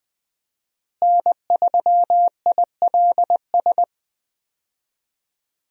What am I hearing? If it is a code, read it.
Morse code, heard as a keyed tone, about 20 wpm, 700 Hz. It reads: N3ILS
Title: USA Call Signs - 5 Characters - Repeated 2x - 20wpm